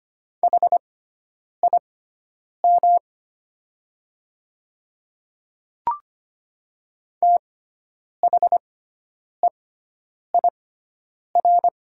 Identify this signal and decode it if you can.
Morse code, heard as a keyed tone, about 25 wpm, spaced as for 9 wpm, 700 Hz. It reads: HIM THEIR
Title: 6x Character Spacing - Top 100 Words - 25wpm